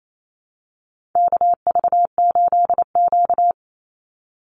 Morse code, heard as a keyed tone, about 28 wpm, 700 Hz. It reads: KV8Q
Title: Call Signs - 28wpm